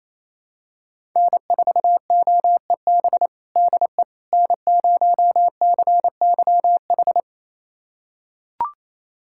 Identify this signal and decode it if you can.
Morse code, heard as a keyed tone, about 28 wpm, 700 Hz. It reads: N4OEB DE N0CYH
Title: USA Callsign DE Callsign 28wpm